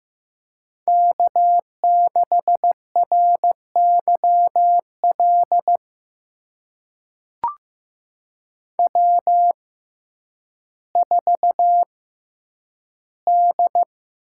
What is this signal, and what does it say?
Morse code, heard as a keyed tone, about 15 wpm, spaced as for 9 wpm, 700 Hz. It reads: K6RYL W4D
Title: USA Call Signs - 5 Characters - 6x Character Spacing - 15wpm